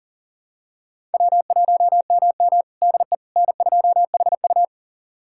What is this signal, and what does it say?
Morse code, heard as a keyed tone, about 37 wpm, 700 Hz. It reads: W1MM DE N2HU
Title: USA Callsign DE Callsign - 4 Characters - 40wpm